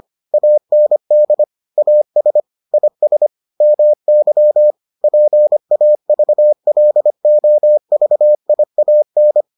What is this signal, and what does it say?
Morse code, heard as a keyed tone, about 25 wpm, 600 Hz. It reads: AND AS IS MY PAVLOVIAN